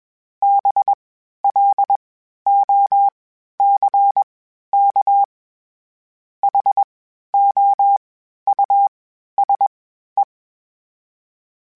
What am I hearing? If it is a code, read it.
Morse code, heard as a keyed tone, about 21 wpm, 800 Hz. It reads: B L O C K H O U S E
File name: General / General-mp3s / Houses @20